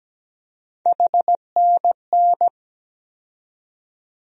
Morse code, heard as a keyed tone, about 17 wpm, 700 Hz. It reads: HNN
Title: USA Call Sign Postfixes - 3 Letters 17wpm